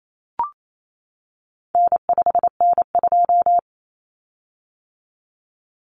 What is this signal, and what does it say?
Morse code, heard as a keyed tone, about 28 wpm, 700 Hz. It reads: N5N2
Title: Common Cut Numbers - 4-Digit 28wpm